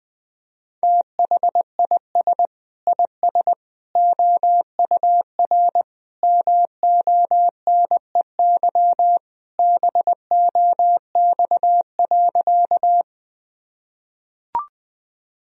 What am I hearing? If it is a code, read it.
Morse code, heard as a keyed tone, about 21 wpm, 700 Hz. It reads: THIS IS OUR MONEY BOX.
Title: Sentences from Top 1000 Words - Review Part 2 - 20wpm